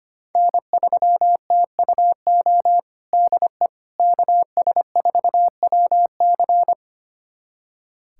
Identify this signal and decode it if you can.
Morse code, heard as a keyed tone, about 25 wpm, 700 Hz. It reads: N3TUO DE KS4WC
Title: USA Callsign DE Callsign - 5 Characters - 25wpm